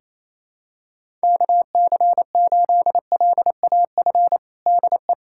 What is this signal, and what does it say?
Morse code, heard as a keyed tone, about 28 wpm, 700 Hz. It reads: KC8LAF DE
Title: USA Callsign DE Callsign - 6 Characters - 28wpm